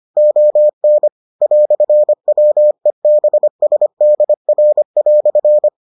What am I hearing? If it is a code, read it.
Morse code, heard as a keyed tone, about 25 wpm, 600 Hz. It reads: ON "WEBSDR"